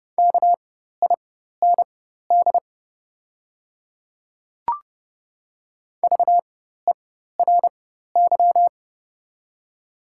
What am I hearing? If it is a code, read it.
Morse code, heard as a keyed tone, about 30 wpm, spaced as for 14 wpm, 700 Hz. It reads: KIND VERY
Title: 4x Character Spacing - Top 200 Words - 30wpm